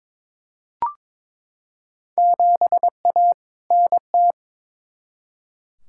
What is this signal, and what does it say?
Morse code, heard as a keyed tone, about 22 wpm, 700 Hz. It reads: 7A NT
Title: Field Day Exchanges Encore - 22wpm